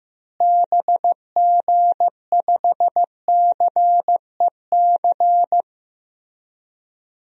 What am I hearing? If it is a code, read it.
Morse code, heard as a keyed tone, about 15 wpm, 700 Hz. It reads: BG5CEC
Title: DXCC-100 Call Signs - 6 Characters - Repeated 3x - 15wpm